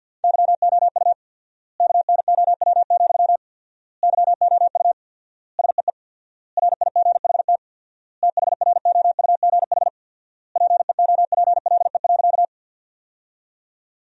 Morse code, heard as a keyed tone, about 50 wpm, 700 Hz. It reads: YOU KNOW, YOU SEE RIGHT THROUGH PEOPLE.
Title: 4x Word-Spacing - Sentences from Top 200 Words - 50wpm